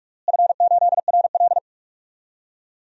Morse code, heard as a keyed tone, about 42 wpm, 700 Hz. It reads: F8RL
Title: DXCC-100 Call Signs - 4 Characters - Repeated 2x - 45wpm